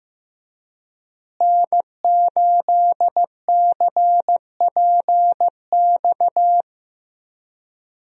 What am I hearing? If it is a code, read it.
Morse code, heard as a keyed tone, about 15 wpm, 700 Hz. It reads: N8CPX
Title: USA Call Signs - 5 Characters - Repeated 3x - 15wpm